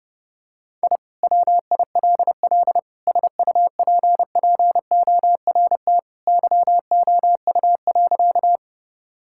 Morse code, heard as a keyed tone, about 30 wpm, 700 Hz. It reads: I WILL SUPPORT YOU.
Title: Sentences from Top 1000 Words - Review Part 5 - 30wpm